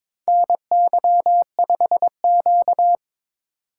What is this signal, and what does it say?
Morse code, heard as a keyed tone, about 22 wpm, 700 Hz. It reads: NY5Q
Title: USA Call Signs - 4 Characters - 4x Character Spacing - 22wpm